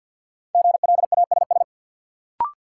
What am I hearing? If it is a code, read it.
Morse code, heard as a keyed tone, about 53 wpm, 700 Hz. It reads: GLASS
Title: Sentences from Top 1300 Words - New Word Drill Alt - 50wpm